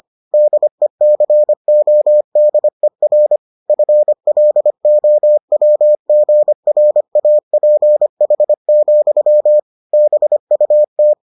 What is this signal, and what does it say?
Morse code, heard as a keyed tone, about 25 wpm, 600 Hz. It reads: DECODER FLOWGRAPH, BUT